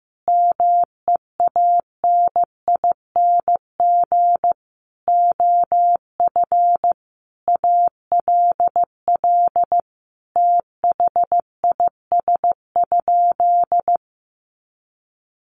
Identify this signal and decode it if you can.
Morse code, heard as a keyed tone, about 15 wpm, 700 Hz. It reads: MEANING OF ALL THIS?
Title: Sentences from Top 900 Words 15wpm